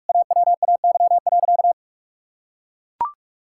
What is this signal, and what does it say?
Morse code, heard as a keyed tone, about 41 wpm, 700 Hz. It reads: AWAY.
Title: Sentences from Top 1000 Words - Review Part 5 - 45wpm